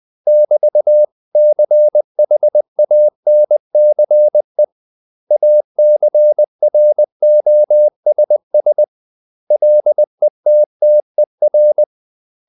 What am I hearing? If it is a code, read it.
Morse code, heard as a keyed tone, about 20 wpm, 600 Hz. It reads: <BT>CHANCE ACROSS LETTER
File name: CW / Practice / 500MostCommonEnglishWords20